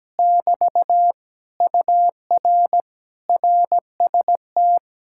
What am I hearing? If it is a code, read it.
Morse code, heard as a keyed tone, about 17 wpm, 700 Hz. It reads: <BT> UR RST
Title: Speed-Racing Common QSO Phrases - 17wpm